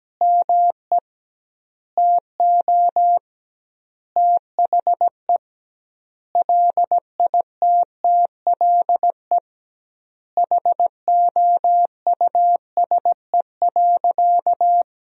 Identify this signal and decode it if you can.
Morse code, heard as a keyed tone, about 17 wpm, 700 Hz. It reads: ME TO THE LITTLE HOUSE.
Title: 2x Word-Spacing - Sentences from Top 200 Words - 17wpm